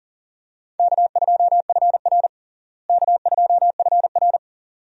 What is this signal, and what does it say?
Morse code, heard as a keyed tone, about 40 wpm, 700 Hz. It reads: K2FR K2FR
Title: USA Call Signs - 4 Characters - Repeated 2x - 40wpm